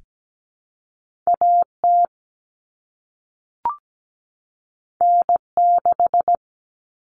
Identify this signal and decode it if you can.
Morse code, heard as a keyed tone, about 17 wpm, 700 Hz. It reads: AT N6
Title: Common Cut Numbers - 2-Digit 17wpm